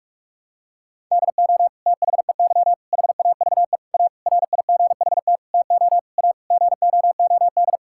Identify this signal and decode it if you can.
Morse code, heard as a keyed tone, about 46 wpm, 700 Hz. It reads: DO THEY HAVE A RIGHT TO A GOOD
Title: Sentences from Top 1000 Words - Review Part 6 - 45wpm